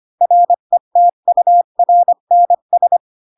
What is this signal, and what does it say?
Morse code, heard as a keyed tone, about 25 wpm, 700 Hz. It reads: RETURNS
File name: Pinocchio / Pinocchio ch7 20 25 ik1hge